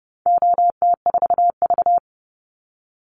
Morse code, heard as a keyed tone, about 30 wpm, 700 Hz. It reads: OT4V
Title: Call Signs - 30wpm